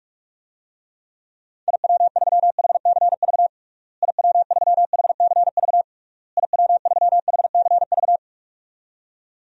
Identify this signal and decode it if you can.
Morse code, heard as a keyed tone, about 46 wpm, 700 Hz. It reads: IW3HCV IW3HCV IW3HCV
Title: DXCC-100 Call Signs - 6 Characters - Repeated 3x - 45wpm